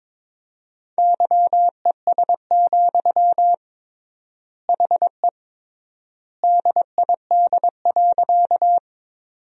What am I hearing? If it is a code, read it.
Morse code, heard as a keyed tone, about 22 wpm, 700 Hz. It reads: YES, HE DID.
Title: 3x Word-Spacing - Sentences from Top 500 Words - 22wpm